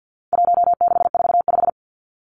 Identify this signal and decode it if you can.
Morse code, heard as a keyed tone, about 50 wpm, 700 Hz. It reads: 1645
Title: Numbers - 4-Digit 50wpm